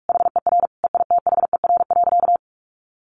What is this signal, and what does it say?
Morse code, heard as a keyed tone, about 46 wpm, 700 Hz. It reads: HER EITHER.
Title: Sentences from Top 1000 Words 45wpm